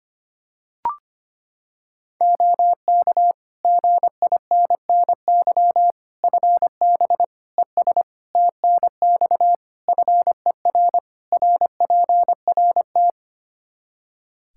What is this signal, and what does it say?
Morse code, heard as a keyed tone, about 25 wpm, 700 Hz. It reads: OK GINNY FB ES TNX FER RPRT
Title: Practice QSO - Exchange 2 - Info - 25wpm